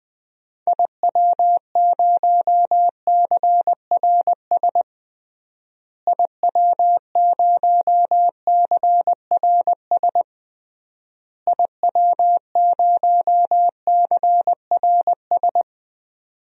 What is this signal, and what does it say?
Morse code, heard as a keyed tone, about 20 wpm, 700 Hz. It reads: IW0CRS IW0CRS IW0CRS
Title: DXCC-100 Call Signs - 6 Characters - Repeated 3x - 20wpm